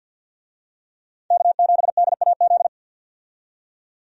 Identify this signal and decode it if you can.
Morse code, heard as a keyed tone, about 52 wpm, 700 Hz. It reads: K7DAZ
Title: USA Call Signs - 5 Characters - 50wpm